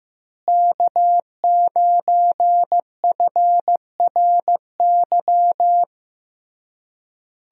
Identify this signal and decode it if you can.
Morse code, heard as a keyed tone, about 15 wpm, 700 Hz. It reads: K9FRY
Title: USA Call Signs - 5 Characters - Repeated 3x - 15wpm